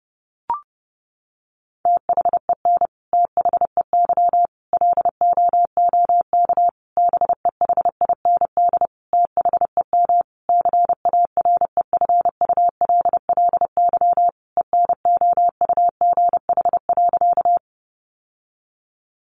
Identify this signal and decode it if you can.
Morse code, heard as a keyed tone, about 30 wpm, 700 Hz. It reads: THEN THEY LOOK BEHIND THEM CAREFULLY ENOUGH.
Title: Sentences from Top 1300 Words 30wpm